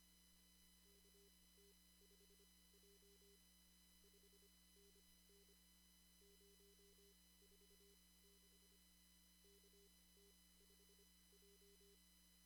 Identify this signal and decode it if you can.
Morse code, heard as a keyed tone, about 25 wpm, spaced as for 18 wpm, 400 Hz. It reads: KT5J 5NN 046 KT5J
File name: CW / kt5j wpx cw-op-k5pi